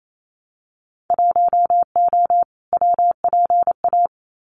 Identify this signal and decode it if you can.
Morse code, heard as a keyed tone, about 28 wpm, 700 Hz. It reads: 1O WPA
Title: Winter Field Day Exchanges 28wpm